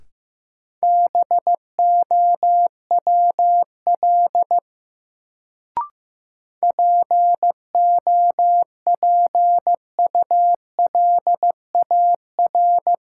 Morse code, heard as a keyed tone, about 15 wpm, 700 Hz. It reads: BOWL POPULAR
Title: Sentences from Top 1200 Words - New Word Drill Alt - 15wpm